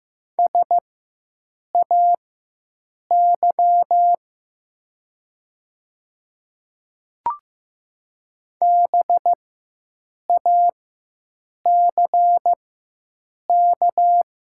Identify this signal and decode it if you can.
Morse code, heard as a keyed tone, about 15 wpm, spaced as for 7 wpm, 700 Hz. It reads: SAY BACK
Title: 4x Character Spacing - Top 300 Words - 15wpm